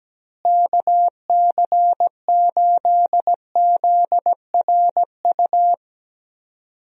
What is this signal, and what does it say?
Morse code, heard as a keyed tone, about 17 wpm, 700 Hz. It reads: KC8ZRU
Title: USA Call Signs - 6 Characters - Repeated 3x - 17wpm